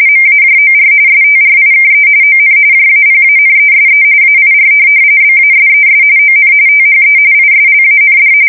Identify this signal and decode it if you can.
RTTY, about 45.45 baud. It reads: MAJOR STRIKE ON JAPANESE AIR BASES ON THE ISLAND
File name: fsk170 / USS-Massachusetts